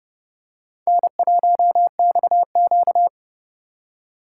Morse code, heard as a keyed tone, about 30 wpm, 700 Hz. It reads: N1XQ
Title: USA Call Signs - 4x Character Spacing - 30wpm